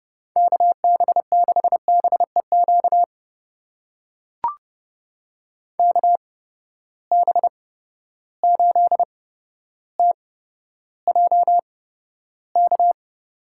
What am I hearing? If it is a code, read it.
Morse code, heard as a keyed tone, about 31 wpm, spaced as for 13 wpm, 700 Hz. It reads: KB6BEQ KB8TJK
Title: USA Call Signs - 6 Characters - 8x Character Spacing - 30wpm